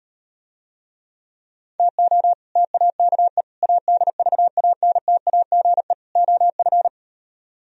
Morse code, heard as a keyed tone, about 38 wpm, 700 Hz. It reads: TO TAKE ADVANTAGE OF